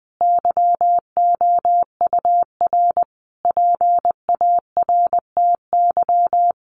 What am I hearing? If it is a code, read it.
Morse code, heard as a keyed tone, about 20 wpm, 700 Hz. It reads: YOUR PARTY
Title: Sentences from Top 1400 Words 20wpm